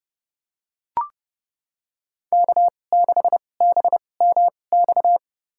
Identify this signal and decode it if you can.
Morse code, heard as a keyed tone, about 30 wpm, spaced as for 22 wpm, 700 Hz. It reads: K6BMX
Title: USA Call Signs - 2x Character Spacing - 30wpm